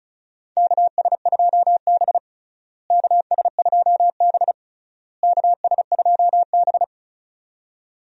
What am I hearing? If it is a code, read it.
Morse code, heard as a keyed tone, about 36 wpm, 700 Hz. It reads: KS2B KS2B KS2B
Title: USA Call Signs - 4 Characters - Repeated 3x - 35wpm